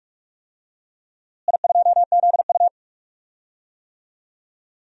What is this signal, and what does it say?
Morse code, heard as a keyed tone, about 46 wpm, 700 Hz. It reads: I2ZU
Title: DXCC-100 Call Signs - 4 Characters - Repeated 3x - 45wpm